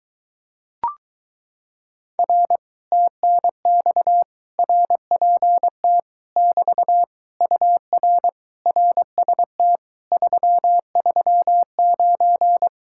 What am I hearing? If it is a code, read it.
Morse code, heard as a keyed tone, about 22 wpm, 700 Hz. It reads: R TNX RPT <BT> UR RST 339